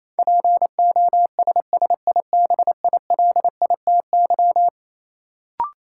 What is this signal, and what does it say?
Morse code, heard as a keyed tone, about 28 wpm, 700 Hz. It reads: POSSIBILITY
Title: Sets of 3 Words - New Word Drill - 28wpm